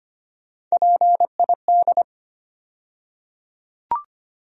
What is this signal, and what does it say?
Morse code, heard as a keyed tone, about 25 wpm, 700 Hz. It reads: PID
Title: USA Call Sign Postfixes - 3 Letters 25wpm